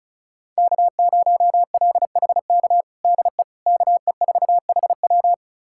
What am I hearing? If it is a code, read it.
Morse code, heard as a keyed tone, about 34 wpm, 700 Hz. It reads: K0LHK DE KE4HW
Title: USA Callsign DE Callsign - 5 Characters - 35wpm